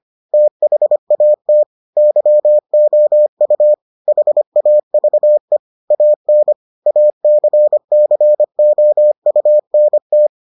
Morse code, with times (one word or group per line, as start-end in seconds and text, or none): 0.3-1.6 THAT
2.0-3.7 YOU
4.1-5.6 HAVE
5.9-6.5 AN
6.9-10.3 ACCOUNT